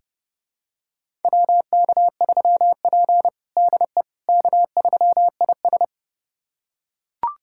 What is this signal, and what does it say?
Morse code, heard as a keyed tone, about 30 wpm, 700 Hz. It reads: WK3P DE K3IS
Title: USA Callsign DE Callsign - 4 Characters - 30wpm